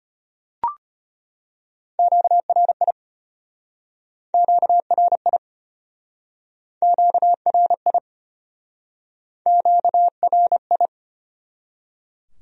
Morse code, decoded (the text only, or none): QRI QRI QRI QRI